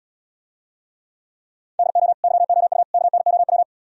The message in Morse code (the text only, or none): DO YOU KNOW